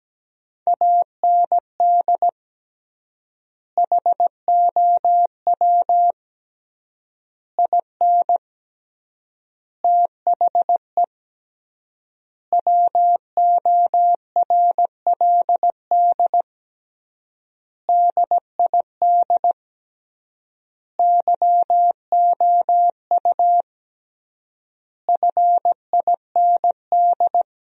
Morse code, heard as a keyed tone, about 17 wpm, 700 Hz. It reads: AND HOW IN THE WORLD DID YOU FIND
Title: 3x Word-Spacing - Sentences from Top 200 Words - 17wpm